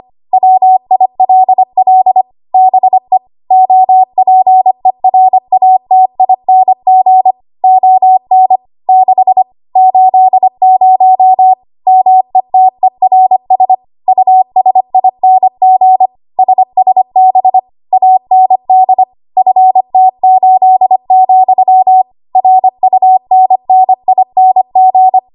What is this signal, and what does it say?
Morse code, heard as a keyed tone, about 25 wpm, 750 Hz. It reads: WILL BE OPERATING ON 6 80 METERS USING SSB AND FT8, RUNNING
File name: Archive / 25 WPM / 25WPM